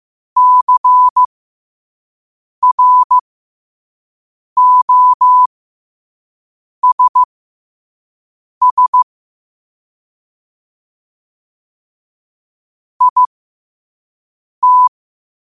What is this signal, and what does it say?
Morse code, heard as a keyed tone, about 15 wpm, spaced as for 5 wpm, 1000 Hz. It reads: CROSS IT
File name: qotdfiles / 5wpm